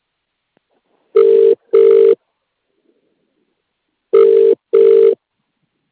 Ringback tone (UK), 400 + 450 Hz.